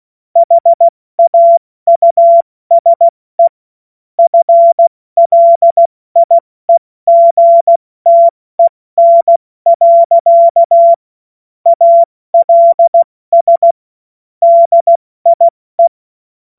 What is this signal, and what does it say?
Morse code, heard as a keyed tone, about 16 wpm, spaced as for 14 wpm, 680 Hz. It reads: HAUSE FLIEGTEN. ALS DIE
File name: Morsekurs / Witze / Witz 16 Januar 2024 70 F80